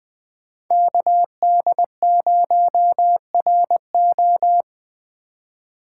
Morse code, heard as a keyed tone, about 20 wpm, 700 Hz. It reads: KD0RO
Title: USA Call Signs - 3x Character Spacing - 20wpm